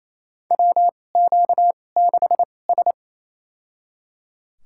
Morse code, heard as a keyed tone, about 28 wpm, spaced as for 20 wpm, 700 Hz. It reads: WQ6S
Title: USA Call Signs - 4 Characters - 2x Character Spacing - 28wpm